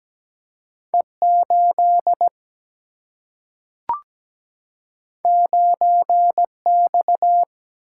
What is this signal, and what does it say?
Morse code, heard as a keyed tone, about 17 wpm, 700 Hz. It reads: E8 9X
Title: Single-Letter Single-Number 17wpm